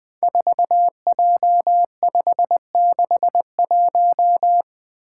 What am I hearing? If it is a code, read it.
Morse code, heard as a keyed tone, about 21 wpm, 700 Hz. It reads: 4J561
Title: Five Challenging Letter-Number Combinations 20wpm